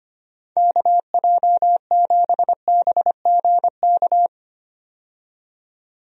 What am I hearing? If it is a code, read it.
Morse code, heard as a keyed tone, about 25 wpm, 700 Hz. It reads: KJ7BGK
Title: USA Call Signs - 6 Characters - 25wpm